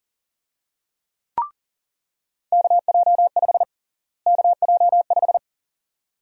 Morse code, heard as a keyed tone, about 40 wpm, 700 Hz. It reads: KJ5 KJ5
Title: USA Call Sign Prefixes - Repeated 2x - 40wpm